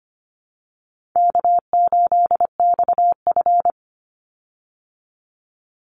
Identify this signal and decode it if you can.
Morse code, heard as a keyed tone, about 25 wpm, 700 Hz. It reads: K8XF
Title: Call Signs - 25wpm